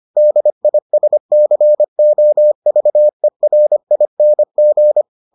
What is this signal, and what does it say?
Morse code, heard as a keyed tone, about 25 wpm, 600 Hz. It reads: DISCOVERING